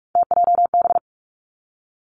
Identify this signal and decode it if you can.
Morse code, heard as a keyed tone, about 46 wpm, 700 Hz. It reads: TJB
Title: Three Challenging Letter-Number Combinations 45wpm